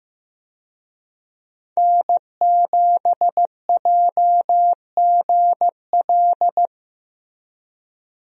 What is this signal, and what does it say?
Morse code, heard as a keyed tone, about 15 wpm, 700 Hz. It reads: N7JGL
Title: USA Call Signs - 5 Characters - Repeated 3x - 15wpm